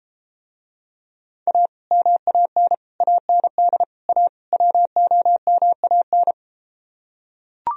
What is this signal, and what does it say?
Morse code, heard as a keyed tone, about 33 wpm, 700 Hz. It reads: A MAN AND A WOMAN